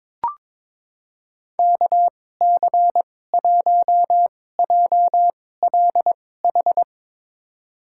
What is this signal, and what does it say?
Morse code, heard as a keyed tone, about 22 wpm, spaced as for 16 wpm, 700 Hz. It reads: KC1JLH
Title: USA Call Signs - 2x Character Spacing - 22wpm